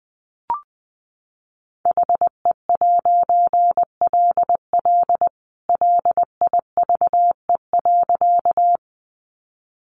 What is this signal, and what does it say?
Morse code, heard as a keyed tone, about 20 wpm, 700 Hz. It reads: HE'LL LIVE.
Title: A Sound Of Thunder 20wpm